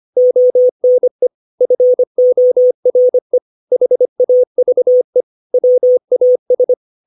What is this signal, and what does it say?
Morse code, heard as a keyed tone, about 25 wpm, 500 Hz. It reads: ONE FORE HAVE WAS
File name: 632 words / 632 Words 25WPM